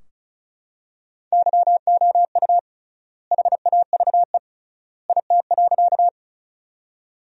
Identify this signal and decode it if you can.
Morse code, heard as a keyed tone, about 34 wpm, 700 Hz. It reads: YOU HAVE IT.
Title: 3x Word-Spacing - Sentences from Top 100 Words - 35wpm